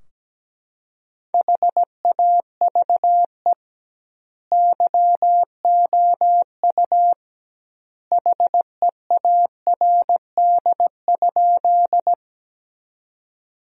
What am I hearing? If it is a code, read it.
Morse code, heard as a keyed tone, about 17 wpm, 700 Hz. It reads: HAVE YOU HEARD?
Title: 2x Word-Spacing - Sentences from Top 400 Words - 17wpm